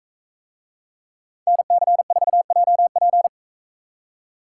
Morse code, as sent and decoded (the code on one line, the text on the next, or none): -. -.-. ....- .--- .--.
NC4JP